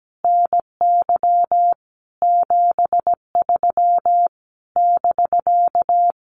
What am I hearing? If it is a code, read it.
Morse code, heard as a keyed tone, about 17 wpm, 700 Hz. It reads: NY 73 <BK>
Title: Common QSO Phrases Encore 17wpm